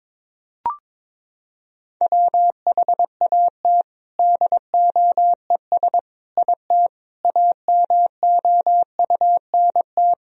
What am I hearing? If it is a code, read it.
Morse code, heard as a keyed tone, about 22 wpm, 700 Hz. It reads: WHAT DOES IT AMOUNT
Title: Sentences from Top 1000 Words - Review Part 4 - 22wpm